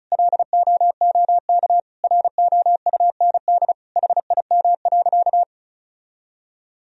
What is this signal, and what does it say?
Morse code, heard as a keyed tone, about 36 wpm, 700 Hz. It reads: LOOK ROUND HIM.
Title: Sentences from Top 1000 Words - Review Part 1 - 35wpm